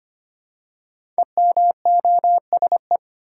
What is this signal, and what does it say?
Morse code, heard as a keyed tone, about 25 wpm, 700 Hz. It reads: EMOSE
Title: Code Groups - 5 Letters - 25wpm